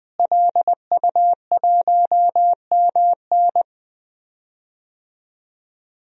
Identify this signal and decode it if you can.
Morse code, heard as a keyed tone, about 20 wpm, 700 Hz. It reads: LU1MN
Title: DXCC-100 Call Signs - 5 Characters - Repeated 2x - 20wpm